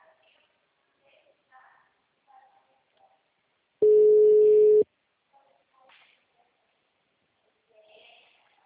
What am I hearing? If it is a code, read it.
Ringback tone (Europe, 425 Hz), with speech.
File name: out- -2001- - - .36